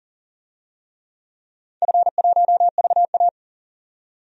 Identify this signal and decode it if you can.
Morse code, heard as a keyed tone, about 38 wpm, 700 Hz. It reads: F1VA
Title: DXCC-100 Call Signs - 4 Characters - 40wpm